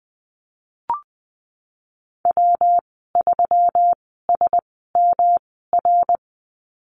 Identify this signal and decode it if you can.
Morse code, heard as a keyed tone, about 20 wpm, spaced as for 15 wpm, 700 Hz. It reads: W3SMR
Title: 122 Lesson 20g - Number 3 - Call Signs 20wpm